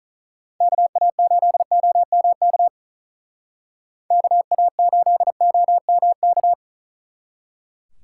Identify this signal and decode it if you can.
Morse code, heard as a keyed tone, about 42 wpm, 700 Hz. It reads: KA8OMK KA8OMK
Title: Speed Racing - USA 2x3 Call Signs 35wpm